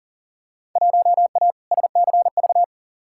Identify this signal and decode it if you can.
Morse code, heard as a keyed tone, about 40 wpm, 700 Hz. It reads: 1A SCV
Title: Field Day Exchanges - 40wpm